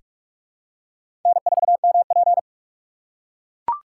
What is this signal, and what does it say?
Morse code, heard as a keyed tone, about 42 wpm, 700 Hz. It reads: N4MP
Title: USA Call Signs - 4 Characters - 2x Character Spacing - 45wpm